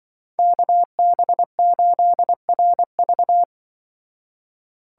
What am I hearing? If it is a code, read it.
Morse code, heard as a keyed tone, about 24 wpm, 700 Hz. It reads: KB8RV
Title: Speed Racing - USA 2x2 Call Signs 20wpm